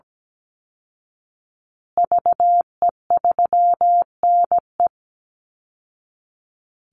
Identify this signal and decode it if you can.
Morse code, heard as a keyed tone, about 17 wpm, 700 Hz. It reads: VE3NE
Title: Call Signs Repeated 2x - 17wpm